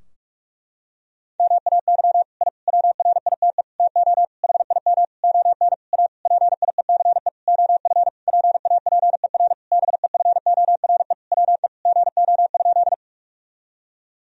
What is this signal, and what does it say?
Morse code, heard as a keyed tone, about 46 wpm, 700 Hz. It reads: MAY I WRITE TO HIM ON A PIECE OF PAPER BEFORE WE GO?
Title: Sentences from Top 1000 Words - Review Part 2 - 45wpm